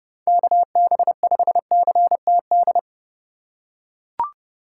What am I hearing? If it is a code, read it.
Morse code, heard as a keyed tone, about 30 wpm, 700 Hz. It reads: KB5CTD